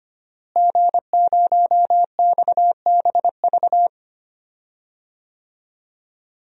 Morse code, heard as a keyed tone, about 25 wpm, 700 Hz. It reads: G0XBV
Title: DXCC-100 Call Signs - 5 Characters - Repeated 3x - 25wpm